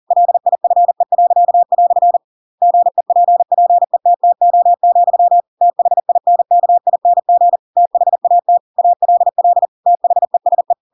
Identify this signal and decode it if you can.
Morse code, heard as a keyed tone, about 40 wpm, 700 Hz. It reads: LIFE." GEPPETTO, THINKING THAT ALL THESE
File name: Pinocchio / Pinocchio ch7 40 40 ik1hge